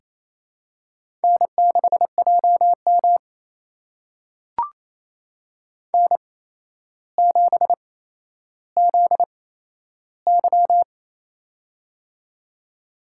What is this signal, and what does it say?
Morse code, heard as a keyed tone, about 28 wpm, spaced as for 12 wpm, 700 Hz. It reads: N6JM N7ZY
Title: USA Call Signs - 4 Characters - 8x Character Spacing - 28wpm